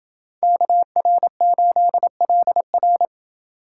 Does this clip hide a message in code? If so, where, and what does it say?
Yes: Morse code from 0.4 to 3.1 s, heard as a keyed tone, about 28 wpm, 700 Hz. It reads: KR8LR